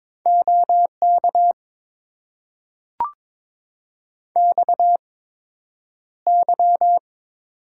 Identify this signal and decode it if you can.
Morse code, heard as a keyed tone, about 22 wpm, spaced as for 9 wpm, 700 Hz. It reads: OK XY
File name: Top 90 QSO Elements - 8x Character Spacing - 22wpm